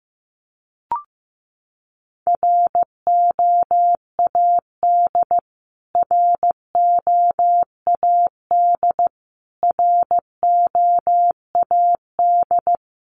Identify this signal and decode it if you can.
Morse code, heard as a keyed tone, about 15 wpm, 700 Hz. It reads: ROAD ROAD ROAD
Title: Top 500 Words Repeated 15wpm